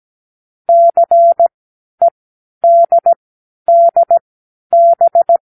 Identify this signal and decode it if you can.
Morse code, heard as a keyed tone, about 17 wpm, spaced as for 10 wpm, 680 Hz. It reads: CEDDB
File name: k7qo / Lesson9(A-F)